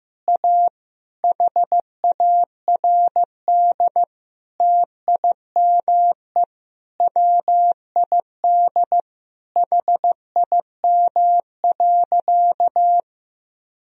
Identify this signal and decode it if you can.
Morse code, heard as a keyed tone, about 15 wpm, 700 Hz. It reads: A HARD TIME WID HIM.
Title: Sentences from Top 300 Words and Abbreviations 15wpm